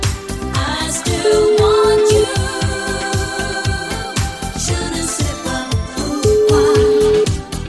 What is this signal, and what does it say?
Ringback tone (Europe, 425 Hz), with noise.